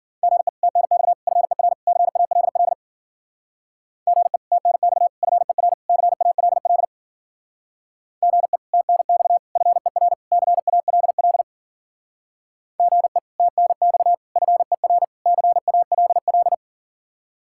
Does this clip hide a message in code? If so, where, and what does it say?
Yes: Morse code from 0.2 to 16.5 s, heard as a keyed tone, about 53 wpm, 700 Hz. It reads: GE TNX FER CALL GE TNX FER CALL GE TNX FER CALL GE TNX FER CALL